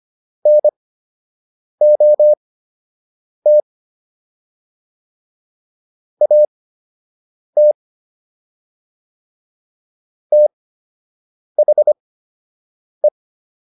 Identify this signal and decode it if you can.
Morse code, heard as a keyed tone, about 25 wpm, spaced as for 7 wpm, 600 Hz. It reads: NOT AT THE